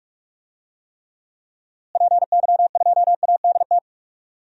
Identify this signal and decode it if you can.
Morse code, heard as a keyed tone, about 46 wpm, 700 Hz. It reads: PY2ADT
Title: DXCC-100 Call Signs - 6 Characters - Repeated 2x - 45wpm